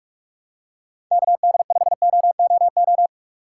Morse code, heard as a keyed tone, about 46 wpm, 700 Hz. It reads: KD5OOO
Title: USA Call Signs - 6 Characters - 3x Character Spacing - 45wpm